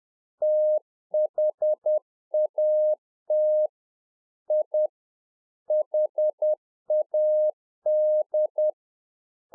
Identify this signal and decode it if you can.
Morse code, heard as a keyed tone, about 10 wpm, 610 Hz. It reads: THAT I HAD